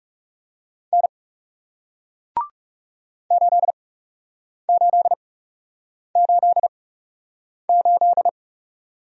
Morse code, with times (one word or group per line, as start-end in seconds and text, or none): 0.9-1.1 N
3.3-8.3 8888